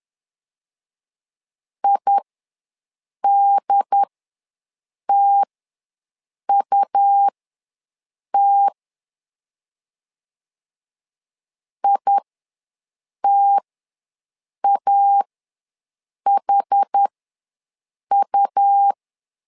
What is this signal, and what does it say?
Morse code, heard as a keyed tone, about 10 wpm, spaced as for 6 wpm, 800 Hz. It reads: IDTUT ITAHU
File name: LEZIONI / 2 UHDN / 2 UHD ws 10wpm 2058